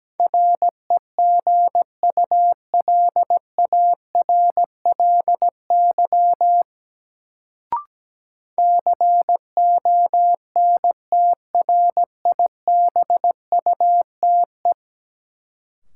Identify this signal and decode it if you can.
Morse code, heard as a keyed tone, about 17 wpm, 700 Hz. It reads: REGULARLY CONTRIBUTE
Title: Sentences from Top 1400 Words - New Word Drill Alt - 17wpm